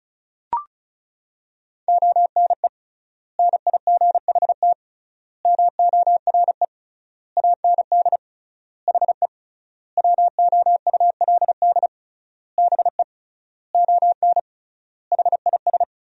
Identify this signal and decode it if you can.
Morse code, heard as a keyed tone, about 36 wpm, 700 Hz. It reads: ONE NIGHT MORE AND HE WOULD BE ON HIS
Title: 3x Word-Spacing - Sentences from Top 300 Words - 35wpm